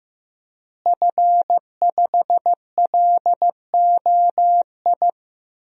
Morse code, heard as a keyed tone, about 15 wpm, 700 Hz. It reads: F5LOI
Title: DXCC-100 Call Signs - 5 Characters - Repeated 3x - 15wpm